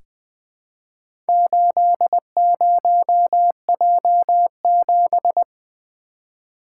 Morse code, heard as a keyed tone, about 20 wpm, 700 Hz. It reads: 80J7
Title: Four Letter-Number Combinations - 20wpm